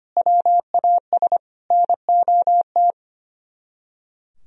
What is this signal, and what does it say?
Morse code, heard as a keyed tone, about 29 wpm, 700 Hz. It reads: WAS NOT
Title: Speed-Racing - Sets of 4 Words 25wpm